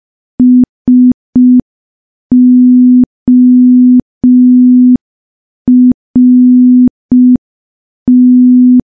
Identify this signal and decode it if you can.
Morse code, heard as a keyed tone, about 5 wpm, 250 Hz. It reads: SORT
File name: morse / morse-training-05WPM